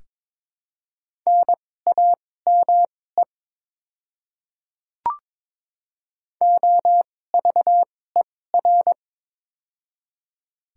Morse code, heard as a keyed tone, about 22 wpm, spaced as for 16 wpm, 700 Hz. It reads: NAME OVER
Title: 2x Character Spacing - Top 500 Words - 22wpm